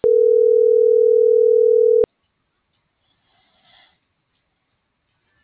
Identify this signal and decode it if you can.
Ringback tone (North America), 440 + 480 Hz.
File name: CENTRAL MULTIMODAL / out cecisandoval - .2794